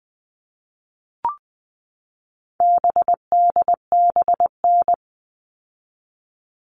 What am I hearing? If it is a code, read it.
Morse code, heard as a keyed tone, about 20 wpm, 700 Hz. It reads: BDBN
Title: All Cut Numbers - 4-Digit 20wpm